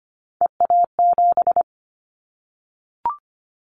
Morse code, heard as a keyed tone, about 25 wpm, 700 Hz. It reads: EA7
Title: Callsign Prefixes for Top 25 DXCC - Repeated 3x - 25wpm